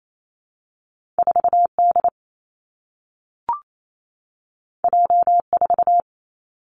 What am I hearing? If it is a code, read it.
Morse code, heard as a keyed tone, about 28 wpm, 700 Hz. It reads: 4 D J 4
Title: Two Challenging Letter-Number Combinations 28wpm